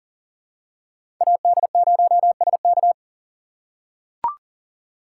Morse code, heard as a keyed tone, about 40 wpm, 700 Hz. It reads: AD0SK
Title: USA Call Signs - 5 Characters - 6x Character Spacing - 40wpm